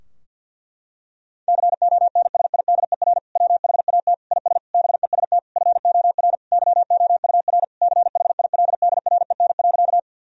Morse code, heard as a keyed tone, about 50 wpm, 700 Hz. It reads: CONSIDER WHAT IS BEST FOR YOUR CHILDREN.
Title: Sentences from Top 1000 Words - Review Part 4 - 50wpm